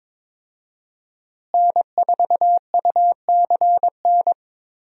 Morse code, heard as a keyed tone, about 21 wpm, 700 Hz. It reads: N4UCN
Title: USA Call Signs - 5 Characters - Repeated 3x - 22wpm